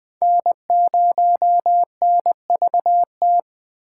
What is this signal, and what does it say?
Morse code, heard as a keyed tone, about 20 wpm, 700 Hz. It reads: N0NVT
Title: USA Call Signs - 5 Characters - 6x Character Spacing - 20wpm